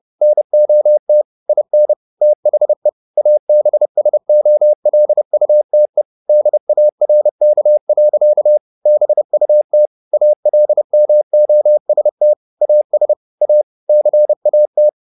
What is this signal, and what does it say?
Morse code, heard as a keyed tone, about 30 wpm, 600 Hz. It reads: NOT IN THE ABSOLUTE DARK. BUT ALMOST AS A CAT
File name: the old man and the sea / the old man and the sea 1 30wpm